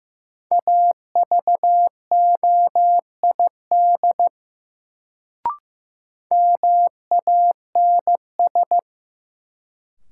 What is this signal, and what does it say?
Morse code, heard as a keyed tone, about 15 wpm, 700 Hz. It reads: AVOID MANS
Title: Sentences from Top 1000 Words - Review Part 6 - New Word Drill - 15wpm